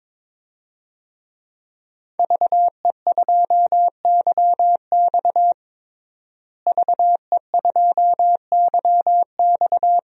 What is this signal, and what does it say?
Morse code, heard as a keyed tone, about 22 wpm, 700 Hz. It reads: VE2YX VE2YX
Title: DXCC-100 Call Signs - 5 Characters - Repeated 2x - 22wpm